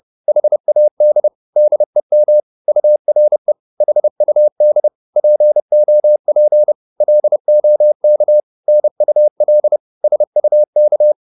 Morse code, heard as a keyed tone, about 30 wpm, 600 Hz. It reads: HAD DEM URE HUD POP LOK NUL SUK
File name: Korte ord DK 30wpm